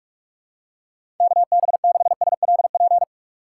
Morse code, heard as a keyed tone, about 47 wpm, 700 Hz. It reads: KB6SLP
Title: USA Call Signs - 6 Characters - 3x Character Spacing - 45wpm